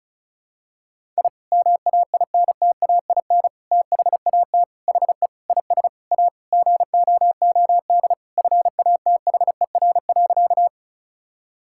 Morse code, heard as a keyed tone, about 36 wpm, 700 Hz. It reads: I MAINTAIN THAT HE IS A GOOD FATHER.
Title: Sentences from Top 1000 Words - Review Part 5 - 35wpm